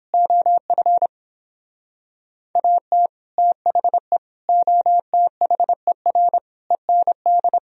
Morse code, heard as a keyed tone, about 31 wpm, 700 Hz. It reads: OF AT THE OTHER END